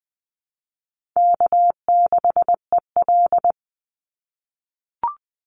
Morse code, heard as a keyed tone, about 20 wpm, 700 Hz. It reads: K6EL
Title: NA SOTA Chaser Callsigns 20wpm